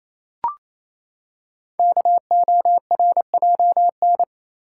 Morse code, heard as a keyed tone, about 27 wpm, 700 Hz. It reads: KORJN
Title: Code Groups - 5 Letters - 28wpm